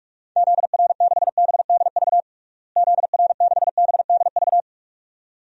Morse code, heard as a keyed tone, about 46 wpm, 700 Hz. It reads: ZR6BDV ZR6BDV
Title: DXCC-100 Call Signs - 6 Characters - Repeated 2x - 45wpm